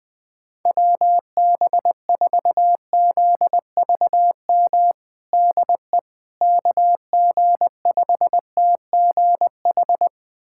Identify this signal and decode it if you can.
Morse code, heard as a keyed tone, about 21 wpm, 700 Hz. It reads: WB4ZVM DE KG5TGH
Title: USA Callsign DE Callsign - 6 Characters - 20wpm